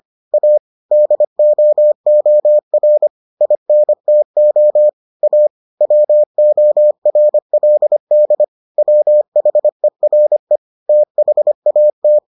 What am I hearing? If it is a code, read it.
Morse code, heard as a keyed tone, about 25 wpm, 600 Hz. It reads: A DOOR INTO A WORLD WHERE THAT